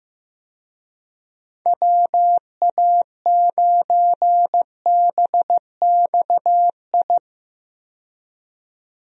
Morse code, heard as a keyed tone, about 16 wpm, 700 Hz. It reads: WA9BXI
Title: USA Call Signs - 6 Characters - 15wpm